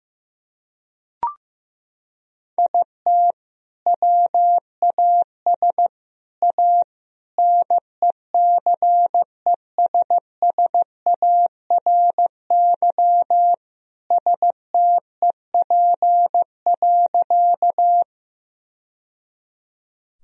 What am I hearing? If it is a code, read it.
Morse code, heard as a keyed tone, about 15 wpm, 700 Hz. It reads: IT WAS A NECESSARY STEP.
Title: Sentences from Top 1000 Words - Review Part 2 - 15wpm